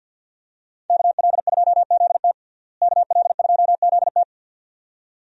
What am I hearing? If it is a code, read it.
Morse code, heard as a keyed tone, about 53 wpm, 700 Hz. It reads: KL2ZT KL2ZT
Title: USA Call Signs - 5 Characters - Repeated 2x - 50wpm